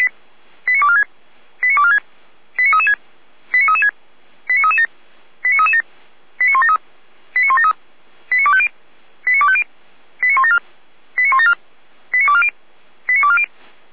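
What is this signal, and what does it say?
Five-tone selective call (ZVEI), heard as quick runs of five tones, tones of about 70 ms. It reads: 89246 89246 89336 89336 89337 89337 89173 89173 89250 89250 89174 89174 89240 89240
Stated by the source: ZVEI five-tone selective call received on an SDR, from the Feuerwehr Orlamünde (fire brigade) audio server.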